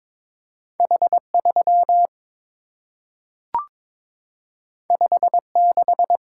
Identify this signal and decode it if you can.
Morse code, heard as a keyed tone, about 22 wpm, 700 Hz. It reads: H3 56
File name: practice / 22wpm / Two Letter-Number Combinations - 22wpm